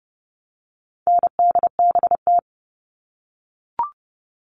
Morse code, heard as a keyed tone, about 30 wpm, 700 Hz. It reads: NDBT
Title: All Cut Numbers - 4-Digit 30wpm